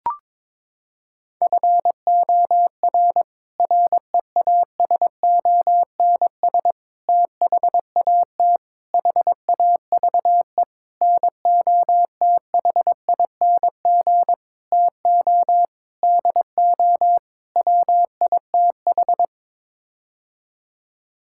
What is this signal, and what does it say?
Morse code, heard as a keyed tone, about 22 wpm, 700 Hz. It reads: FOR REASONS THAT HAVE NOTHING TO DO WITH
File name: practice / 22wpm / Sets of 8 Words - 22wpm